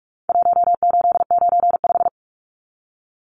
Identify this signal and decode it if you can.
Morse code, heard as a keyed tone, about 44 wpm, 700 Hz. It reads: 1895
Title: Numbers - Contest Style - 4-Digit 45wpm